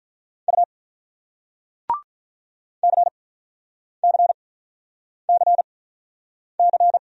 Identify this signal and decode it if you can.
Morse code, heard as a keyed tone, about 48 wpm, spaced as for 9 wpm, 700 Hz. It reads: U CCCC